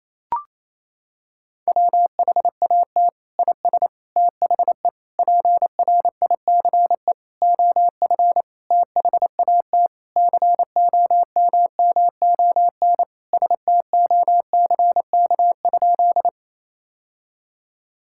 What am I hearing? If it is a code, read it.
Morse code, heard as a keyed tone, about 28 wpm, 700 Hz. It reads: WHAT IS THE PRICE OF THAT COMMON STOCK?
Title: Sentences from Top 1000 Words - Review Part 5 - 28wpm